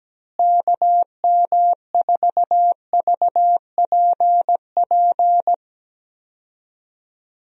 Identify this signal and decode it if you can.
Morse code, heard as a keyed tone, about 17 wpm, 700 Hz. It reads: KM4VPP
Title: USA Call Signs - 6 Characters - Repeated 3x - 17wpm